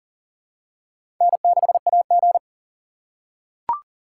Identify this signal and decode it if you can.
Morse code, heard as a keyed tone, about 40 wpm, 700 Hz. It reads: N6AG
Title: USA Call Signs - 4x Character Spacing - 40wpm